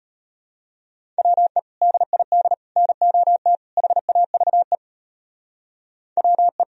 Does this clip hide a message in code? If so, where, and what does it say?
Yes: Morse code from 1.2 to 6.6 s, heard as a keyed tone, about 32 wpm, 700 Hz. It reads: WE DID NOT HAVE WE